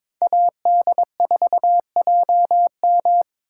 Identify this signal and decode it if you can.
Morse code, heard as a keyed tone, about 26 wpm, 700 Hz. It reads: AD4JM
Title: Speed Racing - USA 2x2 Call Signs 22wpm